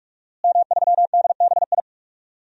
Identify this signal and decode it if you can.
Morse code, heard as a keyed tone, about 46 wpm, 700 Hz. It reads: M3DBI
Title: DXCC-100 Call Signs - 5 Characters - Repeated 2x - 45wpm